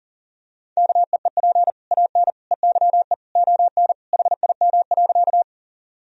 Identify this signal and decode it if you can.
Morse code, heard as a keyed tone, about 37 wpm, 700 Hz. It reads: KEEP AN EYE ON HIM.
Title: Sentences from Top 1000 Words - Review Part 1 - 40wpm